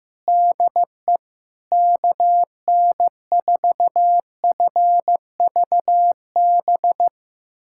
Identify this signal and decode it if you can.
Morse code, heard as a keyed tone, about 16 wpm, 700 Hz. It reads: DE KN4FVB
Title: USA Callsign DE Callsign - 6 Characters - 15wpm